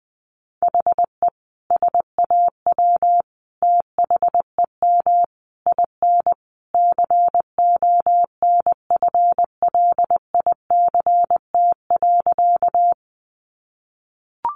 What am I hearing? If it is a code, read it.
Morse code, heard as a keyed tone, about 20 wpm, 700 Hz. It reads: HE SAW THEM IN CONFLICT.
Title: Sentences from Top 1300 Words 20wpm